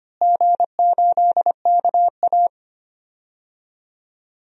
Morse code, heard as a keyed tone, about 25 wpm, 700 Hz. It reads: G8KA
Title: DXCC-100 Call Signs - 4 Characters - Repeated 3x - 25wpm